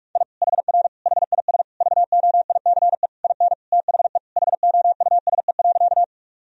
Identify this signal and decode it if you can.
Morse code, heard as a keyed tone, about 46 wpm, 700 Hz. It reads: I HR HIS VOICE IN THE HOUSE.
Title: Sentences from Top 500 Words and Abbreviations 45wpm